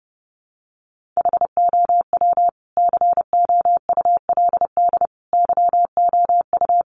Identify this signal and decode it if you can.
Morse code, heard as a keyed tone, about 30 wpm, 700 Hz. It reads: HOW COULD YOU
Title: Sentences from Top 1400 Words 30wpm